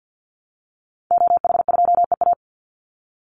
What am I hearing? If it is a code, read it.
Morse code, heard as a keyed tone, about 50 wpm, 700 Hz. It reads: OH2EA
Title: Call Signs Repeated 2x - 50wpm